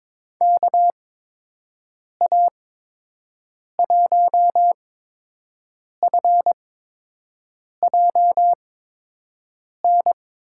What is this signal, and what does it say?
Morse code, heard as a keyed tone, about 22 wpm, spaced as for 6 wpm, 700 Hz. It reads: KA1FJN
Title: USA Call Signs - 6 Characters - 8x Character Spacing - 22wpm